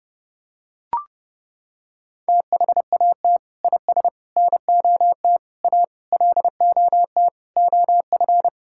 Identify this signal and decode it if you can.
Morse code, heard as a keyed tone, about 30 wpm, 700 Hz. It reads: THAT IS NOT A LOT OF
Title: Sentences from Top 1000 Words - Review Part 5 - 30wpm